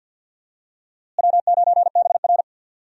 Morse code, heard as a keyed tone, about 46 wpm, 700 Hz. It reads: W9BR
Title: USA Call Signs - 4 Characters - Repeated 3x - 50wpm